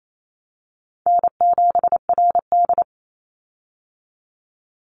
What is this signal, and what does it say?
Morse code, heard as a keyed tone, about 28 wpm, 700 Hz. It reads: N7RD
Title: Easy Call Signs - 28wpm